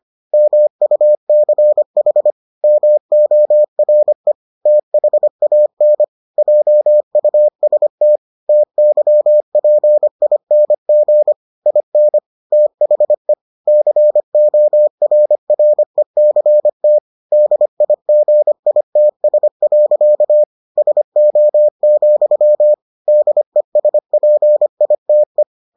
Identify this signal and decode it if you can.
Morse code, heard as a keyed tone, about 25 wpm, 600 Hz. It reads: MUCH MORE THAN JUST TYPING IN THE CORRECT DIGITS. SO, DESPITE